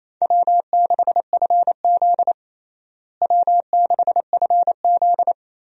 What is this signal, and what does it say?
Morse code, heard as a keyed tone, about 28 wpm, 700 Hz. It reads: W6FZ W6FZ
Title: USA Call Signs - 4 Characters - Repeated 2x - 28wpm